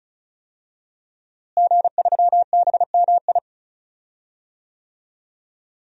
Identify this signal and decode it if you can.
Morse code, heard as a keyed tone, about 36 wpm, 700 Hz. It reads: G3BMI
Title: DXCC-100 Call Signs - 5 Characters - Repeated 2x - 35wpm